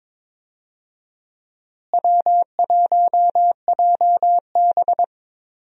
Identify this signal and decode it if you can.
Morse code, heard as a keyed tone, about 22 wpm, 700 Hz. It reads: W1JB
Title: USA Call Signs - 4 Characters - Repeated 2x - 22wpm